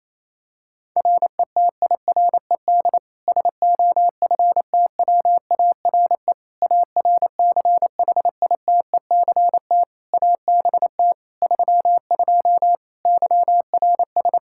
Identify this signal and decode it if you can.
Morse code, heard as a keyed tone, about 28 wpm, 700 Hz. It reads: RETIRED SOFTWARE ARCHITECT ABT 32 YRS
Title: Practice QSO - Exchange 3 - More Info - 28wpm